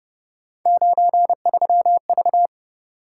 Morse code, heard as a keyed tone, about 29 wpm, 700 Hz. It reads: 93V